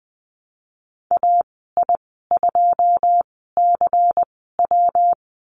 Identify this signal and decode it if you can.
Morse code, heard as a keyed tone, about 20 wpm, spaced as for 14 wpm, 700 Hz. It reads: AI2CW
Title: 122 Lesson 20g - Number 3 - Call Signs 20wpm